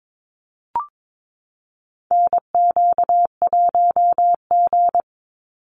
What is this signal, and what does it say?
Morse code, heard as a keyed tone, about 22 wpm, 700 Hz. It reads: NQ1G
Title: USA Call Signs 22wpm